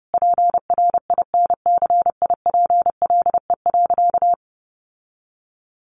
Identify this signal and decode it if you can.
Morse code, heard as a keyed tone, about 30 wpm, 700 Hz. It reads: PRINCIPLE.
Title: Sentences from Top 1400 Words 30wpm